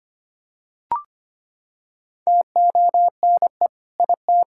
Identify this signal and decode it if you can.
Morse code, heard as a keyed tone, about 25 wpm, 700 Hz. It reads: TONE IT
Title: Sentences from Top 1600 Words 25wpm